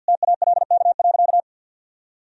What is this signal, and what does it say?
Morse code, heard as a keyed tone, about 47 wpm, 700 Hz. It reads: TALK.
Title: Sentences from Top 1000 Words - Review Part 2 - 50wpm